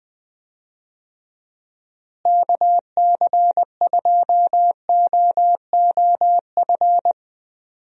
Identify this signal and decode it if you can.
Morse code, heard as a keyed tone, about 20 wpm, 700 Hz. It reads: KC2OOF
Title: USA Call Signs - 6 Characters - Repeated 3x - 20wpm